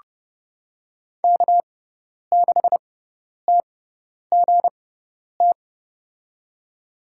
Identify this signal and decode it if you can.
Morse code, heard as a keyed tone, about 32 wpm, spaced as for 10 wpm, 690 Hz. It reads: K6TGT
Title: USA Call Signs - 5 Characters - 6x Character Spacing - 30wpm